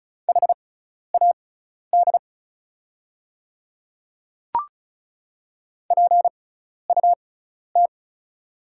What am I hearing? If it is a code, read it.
Morse code, heard as a keyed tone, about 36 wpm, spaced as for 12 wpm, 690 Hz. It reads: HAD PUT
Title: 6x Character Spacing - Top 300 Words - 35wpm